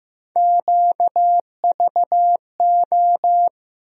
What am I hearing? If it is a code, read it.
Morse code, heard as a keyed tone, about 15 wpm, 700 Hz. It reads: QVO
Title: USA Call Sign Postfixes - 3 Letters 15wpm